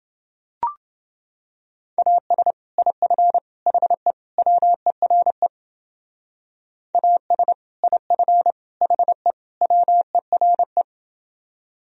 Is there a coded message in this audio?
Yes: Morse code from 2.0 to 10.8 s, heard as a keyed tone, about 30 wpm, 700 Hz. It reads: AS IF HE WERE AS IF HE WERE